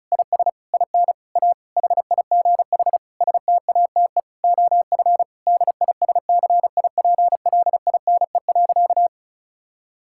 Morse code, heard as a keyed tone, about 36 wpm, 700 Hz. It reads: IS IN A HIGH STATE OF DISCIPLINE.
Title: Sentences from Top 1000 Words - Review Part 6 - 35wpm